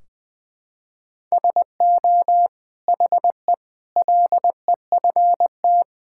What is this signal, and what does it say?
Morse code, heard as a keyed tone, about 21 wpm, 700 Hz. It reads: SO HE LEFT
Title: Sentences from Top 1600 Words 20wpm